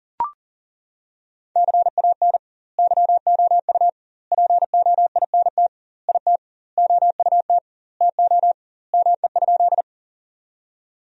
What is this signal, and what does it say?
Morse code, heard as a keyed tone, about 40 wpm, 700 Hz. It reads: CAN YOU POINT IT OUT TO ME?
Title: 2x Word-Spacing - Sentences from Top 200 Words - 40wpm